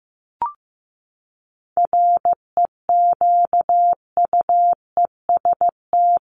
Morse code, heard as a keyed tone, about 14 wpm, 700 Hz. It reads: REQUEST
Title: Top 1751-2000 Words 15wpm